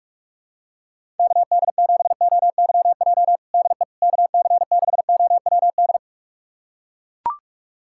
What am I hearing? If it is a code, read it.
Morse code, heard as a keyed tone, about 42 wpm, 700 Hz. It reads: KD7OYJ DE KC6OWD
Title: USA Callsign DE Callsign - 6 Characters - 45wpm